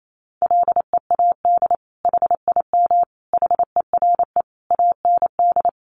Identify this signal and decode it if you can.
Morse code, heard as a keyed tone, about 28 wpm, 700 Hz. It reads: LEAD HIM HERE AND
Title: Sentences from Top 1200 Words 28wpm